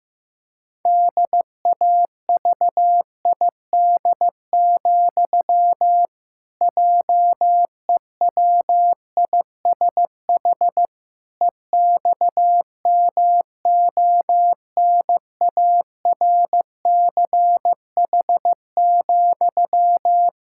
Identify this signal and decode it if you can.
Morse code, heard as a keyed tone, about 16 wpm, 700 Hz. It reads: DAVID, JEWISH EXMONARCH,
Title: Pangram Challenge v2 - 15wpm